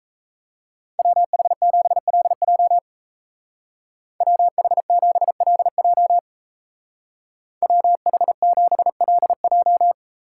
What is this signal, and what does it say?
Morse code, heard as a keyed tone, about 34 wpm, 700 Hz. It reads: WH7LJ WH7LJ WH7LJ
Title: Speed Racing - USA 2x2 Call Signs 28wpm